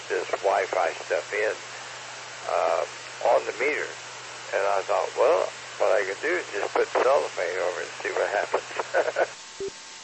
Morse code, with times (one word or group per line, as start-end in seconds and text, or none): none